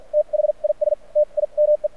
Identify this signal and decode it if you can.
Morse code, heard as a keyed tone, about 50 wpm, spaced as for 40 wpm, 600 Hz. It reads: THIS TIME
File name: archived / CRSnet